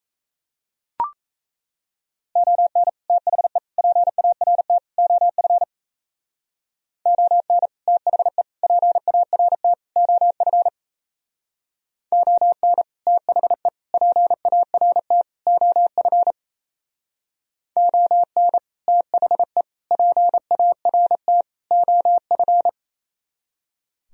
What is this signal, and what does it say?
Morse code, heard as a keyed tone, about 34 wpm, 700 Hz. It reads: ON THE PART OF ON THE PART OF ON THE PART OF ON THE PART OF
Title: Speed-Racing - Sets of 4 Words 28wpm